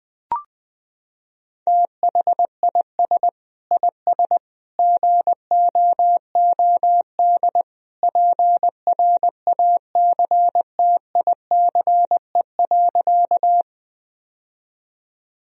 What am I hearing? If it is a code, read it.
Morse code, heard as a keyed tone, about 20 wpm, 700 Hz. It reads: THIS IS GOOD PRACTICE.
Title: Sentences from Top 1000 Words - Review Part 4 - 20wpm